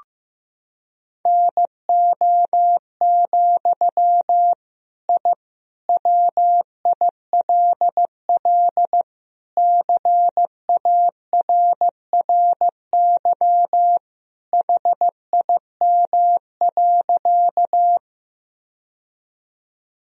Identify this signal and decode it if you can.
Morse code, heard as a keyed tone, about 15 wpm, 700 Hz. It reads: NO, I WILL CARRY HIM.
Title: Sentences from Top 1000 Words - Review Part 1 - 15wpm